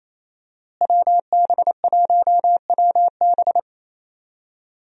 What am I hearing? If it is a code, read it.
Morse code, heard as a keyed tone, about 28 wpm, 700 Hz. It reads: WB1WB
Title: USA Call Signs - 5 Characters - Repeated 2x - 28wpm